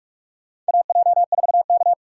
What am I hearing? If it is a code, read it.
Morse code, heard as a keyed tone, about 43 wpm, 700 Hz. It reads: AJ4K
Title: USA Call Signs - 8x Character Spacing - 45wpm